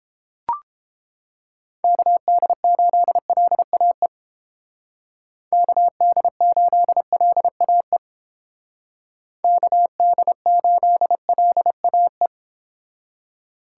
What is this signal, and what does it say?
Morse code, heard as a keyed tone, about 30 wpm, 700 Hz. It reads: KD8LAE KD8LAE KD8LAE